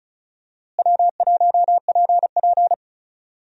Morse code, heard as a keyed tone, about 36 wpm, 700 Hz. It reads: W1PP
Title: USA Call Signs - 4 Characters - Repeated 2x - 35wpm